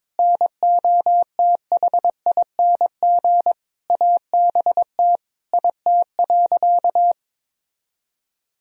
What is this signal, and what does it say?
Morse code, heard as a keyed tone, about 22 wpm, 700 Hz. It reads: NOTHING ABT IT.
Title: Sentences from Top 500 Words and Abbreviations 22wpm